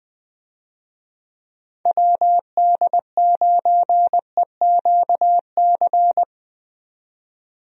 Morse code, heard as a keyed tone, about 21 wpm, 700 Hz. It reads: WD9EQC
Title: USA Call Signs - 6 Characters - 20wpm